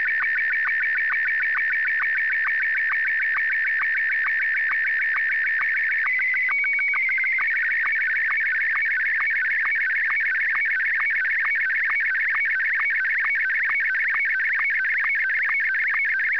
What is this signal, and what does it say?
SSTV (slow-scan television), Martin 1.
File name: article211 / sstv demo